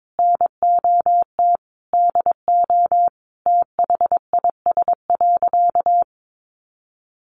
Morse code, heard as a keyed tone, about 22 wpm, 700 Hz. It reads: NOT DO THIS.
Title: Sentences from Top 1100 Words 22wpm